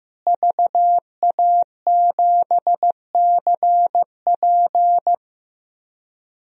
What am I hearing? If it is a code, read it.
Morse code, heard as a keyed tone, about 16 wpm, 700 Hz. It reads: VA7CP
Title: DXCC-100 Call Signs - 5 Characters - Repeated 3x - 15wpm